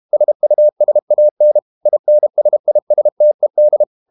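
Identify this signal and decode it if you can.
Morse code, heard as a keyed tone, about 32 wpm, 600 Hz. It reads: SUSAN INSISTED
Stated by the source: Morse code practice at 32 wpm from CW med Gnister, reading Ken Follett's novel Never in English.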